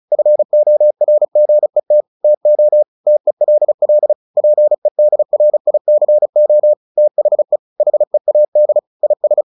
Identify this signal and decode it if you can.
Morse code, heard as a keyed tone, about 34 wpm, 600 Hz. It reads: FORGET TO TELL PEDRICO THE HEAD IS
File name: the old man and the sea 2 35wpm